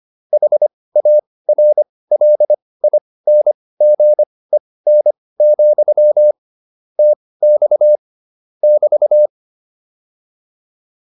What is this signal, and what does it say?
Morse code, heard as a keyed tone, about 25 wpm, spaced as for 18 wpm, 600 Hz. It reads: HARLINGEN, TX <BT>